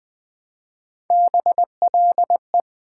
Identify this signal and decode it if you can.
Morse code, heard as a keyed tone, about 20 wpm, 700 Hz. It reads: BLE
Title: Build A Word - 1 Letter At A Time - 5-Letter Words - 20wpm